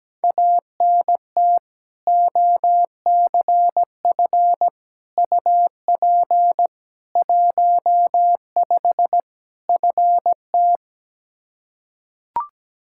Morse code, heard as a keyed tone, about 17 wpm, 700 Hz. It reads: ANT OCF UP 15 FT
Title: Practice QSO - Exchange 2 - Info - 17wpm